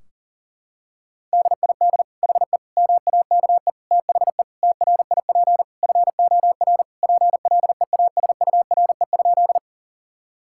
Morse code, heard as a keyed tone, about 40 wpm, 700 Hz. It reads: DID HE MAKE THE TRIP FOR PLEASURE?
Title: Sentences from Top 1600 Words 40wpm